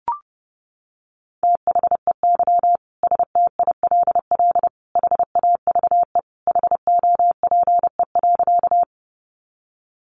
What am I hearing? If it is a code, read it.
Morse code, heard as a keyed tone, about 30 wpm, 700 Hz. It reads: THEY STILL HAVE HOPE.
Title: Sentences from Top 1000 Words 30wpm